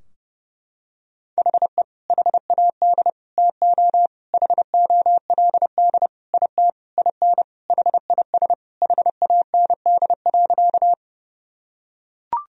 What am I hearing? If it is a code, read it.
Morse code, heard as a keyed tone, about 30 wpm, 700 Hz. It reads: HE HAD TO HOLD IT IN HIS HAND.
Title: Sentences from Top 1000 Words - Review Part 2 - 30wpm